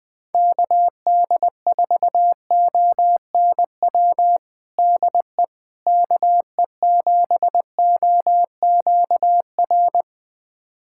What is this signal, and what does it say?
Morse code, heard as a keyed tone, about 20 wpm, 700 Hz. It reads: KD4ONW DE KE7OQR
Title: USA Callsign DE Callsign - 6 Characters - 20wpm